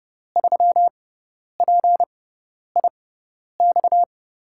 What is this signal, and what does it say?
Morse code, heard as a keyed tone, about 30 wpm, spaced as for 10 wpm, 700 Hz. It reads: 3PIX
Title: USA Call Signs - 5 Characters - 6x Character Spacing - 30wpm